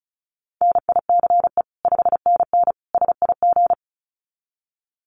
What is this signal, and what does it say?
Morse code, heard as a keyed tone, about 34 wpm, 700 Hz. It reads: NICE 5NN SIG
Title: Common QSO Phrases 35wpm